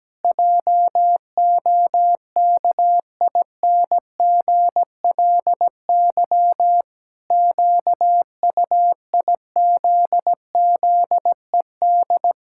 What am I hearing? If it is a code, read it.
Morse code, heard as a keyed tone, about 17 wpm, 700 Hz. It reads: JOKINGLY QUIZZED
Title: Pangram Challenge v2 - 17wpm